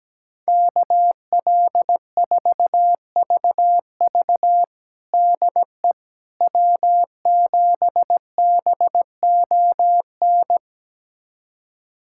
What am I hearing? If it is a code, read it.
Morse code, heard as a keyed tone, about 17 wpm, 700 Hz. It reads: KL4VV DE W7BON
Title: USA Callsign DE Callsign - 5 Characters - 17wpm